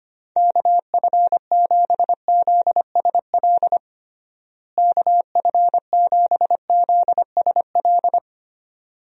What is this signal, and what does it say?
Morse code, heard as a keyed tone, about 25 wpm, 700 Hz. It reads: KF7ZSL KF7ZSL
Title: USA Call Signs - 6 Characters - Repeated 2x - 25wpm